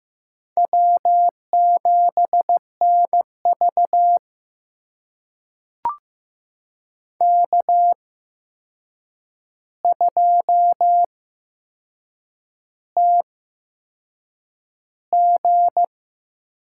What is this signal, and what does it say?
Morse code, heard as a keyed tone, about 15 wpm, spaced as for 6 wpm, 700 Hz. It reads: W7NV K2TG
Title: USA Call Signs - 4 Characters - 8x Character Spacing - 15wpm